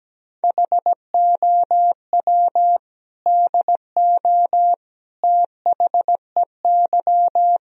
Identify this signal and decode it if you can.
Morse code, heard as a keyed tone, about 17 wpm, 700 Hz. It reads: HOW DO THEY
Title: Sentences from Top 1000 Words - Review Part 2 - 17wpm